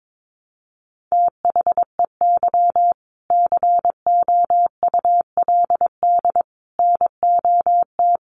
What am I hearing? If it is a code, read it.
Morse code, heard as a keyed tone, about 22 wpm, 700 Hz. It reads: THEY COULD NOT
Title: Sentences from Top 1100 Words 22wpm